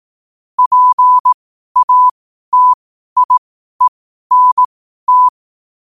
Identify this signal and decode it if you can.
Morse code, heard as a keyed tone, about 18 wpm, spaced as for 12 wpm, 1000 Hz. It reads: PATIENT
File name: qotdfiles / 12wpm